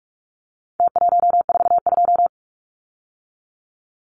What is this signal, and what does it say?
Morse code, heard as a keyed tone, about 44 wpm, 700 Hz. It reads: T142
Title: Numbers - Contest Style - 4-Digit 45wpm